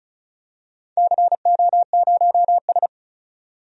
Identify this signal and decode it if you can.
Morse code, heard as a keyed tone, about 36 wpm, 700 Hz. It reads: CO0S
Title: DXCC-100 Call Signs - 4 Characters - 35wpm